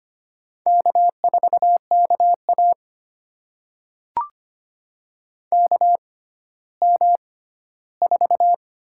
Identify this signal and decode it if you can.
Morse code, heard as a keyed tone, about 25 wpm, spaced as for 14 wpm, 700 Hz. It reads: K4KA KM4
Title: USA Call Signs - 6x Character Spacing - 25wpm